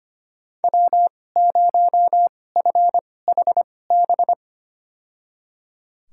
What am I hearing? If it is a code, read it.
Morse code, heard as a keyed tone, about 25 wpm, spaced as for 18 wpm, 700 Hz. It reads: W0FHB
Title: USA Call Signs - 5 Characters - 2x Character Spacing - 25wpm